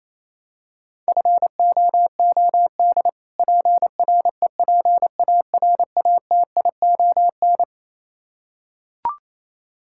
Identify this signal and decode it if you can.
Morse code, heard as a keyed tone, about 28 wpm, 700 Hz. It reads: FOOD PREPARATION
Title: Practice QSO - Exchange 3 - More Info - 28wpm